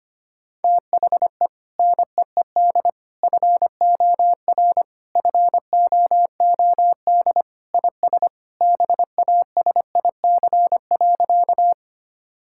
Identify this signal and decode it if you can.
Morse code, heard as a keyed tone, about 25 wpm, 700 Hz. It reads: THE NEED FOR FOOD IS BASIC.
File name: practice / 25wpm / Sentences with 3-Word Phrases - 25wpm